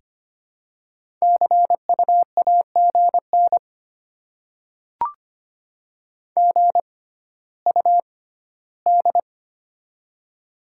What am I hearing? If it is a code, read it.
Morse code, heard as a keyed tone, about 25 wpm, spaced as for 14 wpm, 700 Hz. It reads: CU AGN GUD
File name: Top 20 QSO Elements - 6x Character Spacing - 25wpm